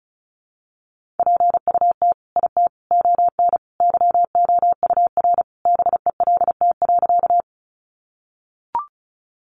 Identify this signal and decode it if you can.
Morse code, heard as a keyed tone, about 34 wpm, 700 Hz. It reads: PUT IT ON YOUR BELT.
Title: Sentences from Top 1400 Words 35wpm